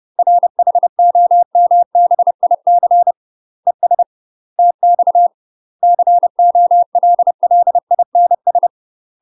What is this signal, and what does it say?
Morse code, heard as a keyed tone, about 30 wpm, 700 Hz. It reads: RHOMBIC ES TX COLLINS
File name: UK MASTER QSOs TRAINING / 30WPM / UK MASTER QSOs 0020